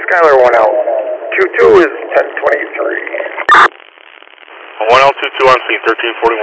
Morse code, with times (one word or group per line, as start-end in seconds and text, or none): none